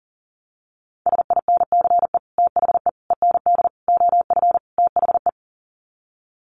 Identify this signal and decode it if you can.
Morse code, heard as a keyed tone, about 37 wpm, 700 Hz. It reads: SINCE THE END OF THE
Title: Sets of 5 Words Encore 40wpm